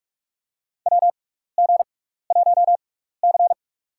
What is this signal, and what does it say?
Morse code, heard as a keyed tone, about 44 wpm, spaced as for 15 wpm, 700 Hz. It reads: WG1C
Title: USA Call Signs - 4 Characters - 6x Character Spacing - 45wpm